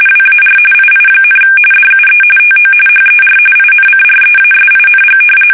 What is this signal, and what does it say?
RTTY, about 45.45 baud. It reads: WAR II, DUE TO HER INCREASING
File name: fsk850 / battleshipnj